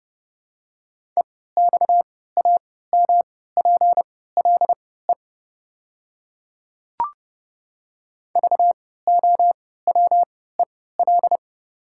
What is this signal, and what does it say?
Morse code, heard as a keyed tone, about 30 wpm, spaced as for 17 wpm, 700 Hz. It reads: EXAMPLE VOWEL
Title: 3x Character Spacing - Top 500 Words - 30wpm